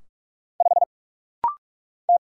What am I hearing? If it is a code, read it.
Morse code, heard as a keyed tone, about 46 wpm, 700 Hz. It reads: 5 T
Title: Single Letter-Number - Rapid-Fire 45wpm